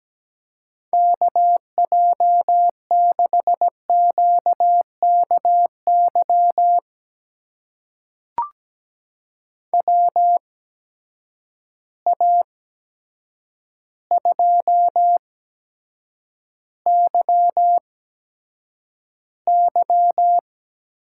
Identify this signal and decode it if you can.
Morse code, heard as a keyed tone, about 17 wpm, spaced as for 8 wpm, 700 Hz. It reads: KJ6QKY WA2YY
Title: USA Call Signs - 8x Character Spacing - 17wpm